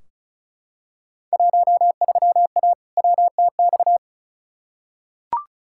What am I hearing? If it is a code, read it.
Morse code, heard as a keyed tone, about 34 wpm, 700 Hz. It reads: 13A WTX
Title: Field Day Exchanges Encore - 35wpm